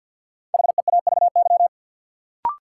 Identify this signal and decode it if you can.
Morse code, heard as a keyed tone, about 49 wpm, 700 Hz. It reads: HEAVY
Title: Sentences from Top 1000 Words - Review Part 6 - New Word Drill - 50wpm